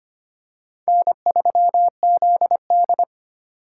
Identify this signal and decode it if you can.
Morse code, heard as a keyed tone, about 25 wpm, 700 Hz. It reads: N3ZD
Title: USA Call Signs - 4 Characters - 3x Character Spacing - 25wpm